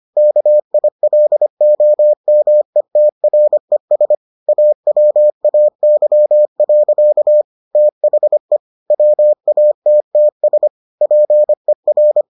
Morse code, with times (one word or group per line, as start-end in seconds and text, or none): 0.2-4.2 KILOMETRES
4.5-7.4 AWAY.
7.8-8.6 THE
8.9-10.7 WATTS
11.0-12.2 PER